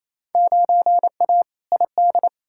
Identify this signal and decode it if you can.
Morse code, heard as a keyed tone, about 28 wpm, 700 Hz. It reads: 9A ID
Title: Field Day Exchanges Encore - 28wpm